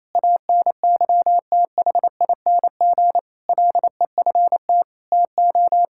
Morse code, heard as a keyed tone, about 28 wpm, 700 Hz. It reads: ANYTHING LEFT TO
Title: Sentences from Top 1000 Words - Review Part 6 - 28wpm